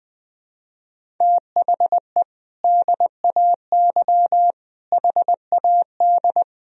Morse code, heard as a keyed tone, about 20 wpm, 700 Hz. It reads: THE DAY HAD
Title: Sentences from Top 1000 Words - Review Part 3 - 20wpm